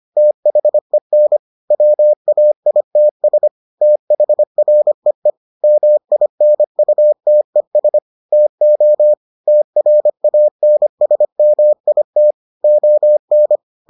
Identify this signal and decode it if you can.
Morse code, heard as a keyed tone, about 25 wpm, 600 Hz. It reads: THEN WAITS THREE MINUTES TO TRANSMIT ON